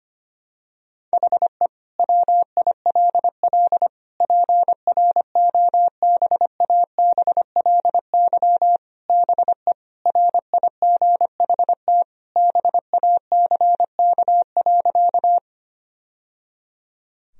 Morse code, heard as a keyed tone, about 25 wpm, 700 Hz. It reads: HE WILL PROBABLY BE RIGHT BACK.
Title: Sentences from Top 1000 Words - Review Part 4 - 25wpm